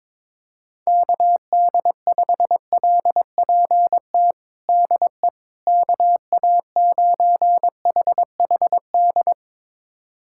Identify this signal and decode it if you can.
Morse code, heard as a keyed tone, about 22 wpm, 700 Hz. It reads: KD5LPT DE KA9HHD
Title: USA Callsign DE Callsign 22wpm